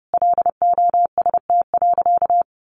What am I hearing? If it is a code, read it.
Morse code, heard as a keyed tone, about 30 wpm, 700 Hz. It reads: LOST.
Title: Sentences from Top 1400 Words 30wpm